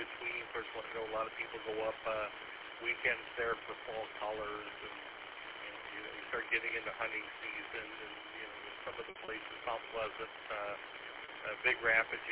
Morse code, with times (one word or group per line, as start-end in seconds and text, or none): none